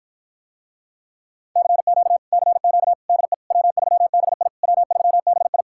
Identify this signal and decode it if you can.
Morse code, heard as a keyed tone, about 53 wpm, 700 Hz. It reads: CQ CQ DE W3BI W3BI